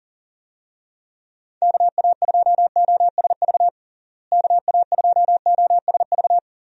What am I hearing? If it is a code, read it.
Morse code, heard as a keyed tone, about 37 wpm, 700 Hz. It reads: KA2OSV KA2OSV
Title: USA Call Signs - 6 Characters - Repeated 2x - 40wpm